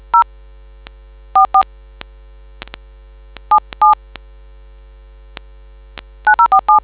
DTMF, heard as telephone touch tones, tones of about 100 ms. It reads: 0 44 77 9047